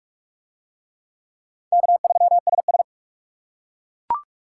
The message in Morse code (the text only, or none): K3SS